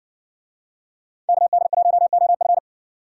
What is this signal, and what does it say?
Morse code, heard as a keyed tone, about 55 wpm, 700 Hz. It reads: KD1OF